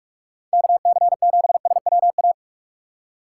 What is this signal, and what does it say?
Morse code, heard as a keyed tone, about 42 wpm, 700 Hz. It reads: KC7SWA
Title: USA Call Signs - 6 Characters - 3x Character Spacing - 45wpm